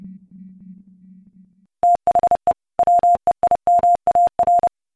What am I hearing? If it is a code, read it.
Morse code, heard as a keyed tone, about 30 wpm, 700 Hz. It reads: THE WEIMAR
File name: MorseCodeChoir